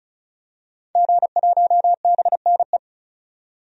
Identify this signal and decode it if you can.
Morse code, heard as a keyed tone, about 34 wpm, 700 Hz. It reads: G1BNE
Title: DXCC-100 Call Signs - 5 Characters - 35wpm